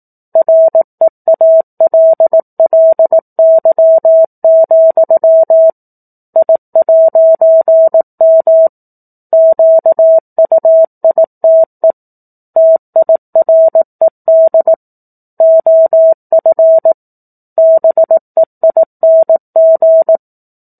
Morse code, heard as a keyed tone, about 18 wpm, 650 Hz. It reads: REALLY, I'M QUITE TIRED OF BEING